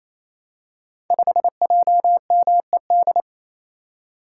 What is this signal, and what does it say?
Morse code, heard as a keyed tone, about 28 wpm, 700 Hz. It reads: 5JMED
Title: Code Groups - 5 Letter:Numbers - 28wpm